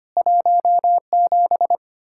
Morse code, heard as a keyed tone, about 32 wpm, 700 Hz. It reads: 17